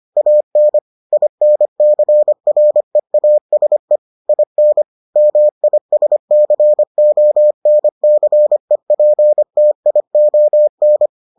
Morse code, heard as a keyed tone, about 25 wpm, 600 Hz. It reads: AN INCREASE IN MISCONCEPTION